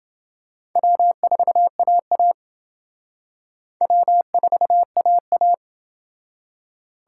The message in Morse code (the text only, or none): W4AA W4AA